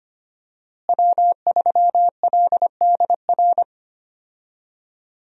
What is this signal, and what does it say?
Morse code, heard as a keyed tone, about 25 wpm, 700 Hz. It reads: W3LDR
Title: USA Call Signs - 5 Characters - Repeated 3x - 25wpm